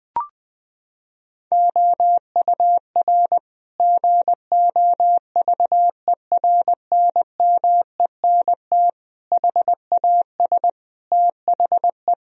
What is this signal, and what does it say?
Morse code, heard as a keyed tone, about 20 wpm, 700 Hz. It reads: OUR GOVERNMENT HAS THE
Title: Sentences from Top 1000 Words - Review Part 3 - 20wpm